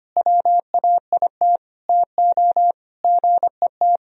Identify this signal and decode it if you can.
Morse code, heard as a keyed tone, about 25 wpm, 700 Hz. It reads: WAIT TO GET
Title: Sentences from Top 500 Words and Abbreviations 25wpm